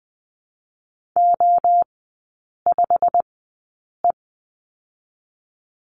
Morse code, heard as a keyed tone, about 20 wpm, spaced as for 8 wpm, 700 Hz. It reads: O5E
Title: 087 Lesson 15e - Number 5 - Three Characters 20wpm